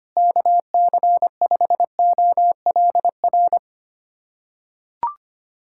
Morse code, heard as a keyed tone, about 25 wpm, 700 Hz. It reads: KC5OLR
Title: USA Callsign DE Callsign - 6 Characters - 25wpm